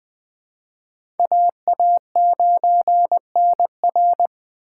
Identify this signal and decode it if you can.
Morse code, heard as a keyed tone, about 20 wpm, 700 Hz. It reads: AA9NR
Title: USA Call Signs - 5 Characters - 6x Character Spacing - 20wpm